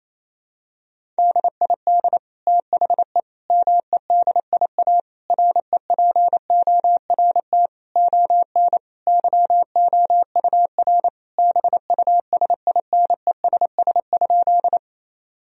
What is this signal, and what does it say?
Morse code, heard as a keyed tone, about 28 wpm, 700 Hz. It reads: DID THE MEDIA REPORT ON YOUR BUSINESS?
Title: Sentences from Top 1000 Words - Review Part 4 - 28wpm